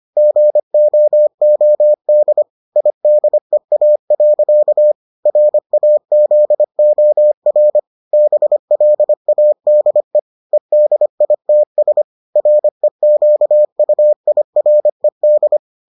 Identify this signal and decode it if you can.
Morse code, heard as a keyed tone, about 25 wpm, 600 Hz. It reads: GOOD IDEA. RAZOR BLADE EDITS REQUIRED